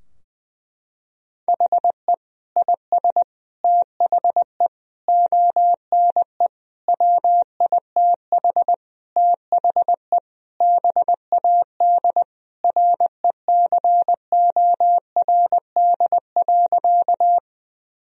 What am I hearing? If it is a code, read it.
Morse code, heard as a keyed tone, about 20 wpm, 700 Hz. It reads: HE IS THE ONE WITH THE BAD RECORD.
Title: Sentences from Top 1000 Words - Review Part 3 - 20wpm